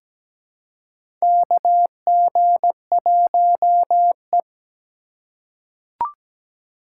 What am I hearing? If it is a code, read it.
Morse code, heard as a keyed tone, about 17 wpm, 700 Hz. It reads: KG1E
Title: USA Call Signs - 4 Characters - 2x Character Spacing - 17wpm